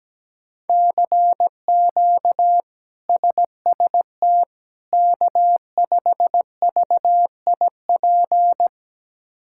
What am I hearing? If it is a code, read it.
Morse code, heard as a keyed tone, about 17 wpm, 700 Hz. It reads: CQ SST K5VIP
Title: K1USN SST QSO Practice - 17wpm